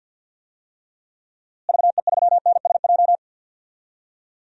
Morse code, heard as a keyed tone, about 52 wpm, 700 Hz. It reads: VE3NSJ
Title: DXCC-100 Call Signs - 6 Characters - Repeated 3x - 50wpm